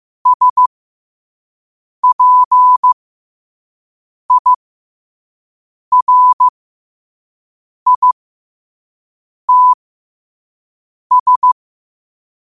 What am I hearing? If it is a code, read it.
Morse code, heard as a keyed tone, about 15 wpm, spaced as for 5 wpm, 1000 Hz. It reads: SPIRITS
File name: qotdfiles / 5wpm